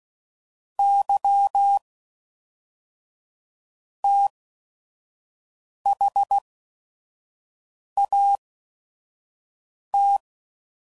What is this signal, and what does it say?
Morse code, heard as a keyed tone, about 16 wpm, spaced as for 5 wpm, 800 Hz. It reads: Y THAT
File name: CW Code Groups @5WPM / Technical Subjects / AND Gate